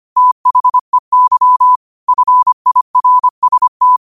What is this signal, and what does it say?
Morse code, heard as a keyed tone, about 25 wpm, 1000 Hz. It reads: THEY FIRST